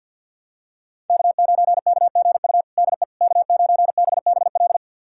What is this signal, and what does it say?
Morse code, heard as a keyed tone, about 52 wpm, 700 Hz. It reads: K9KGU DE K9BBL
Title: USA Callsign DE Callsign - 5 Characters - 50wpm